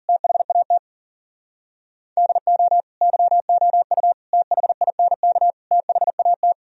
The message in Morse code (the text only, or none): THAT DO YOU THINK THAT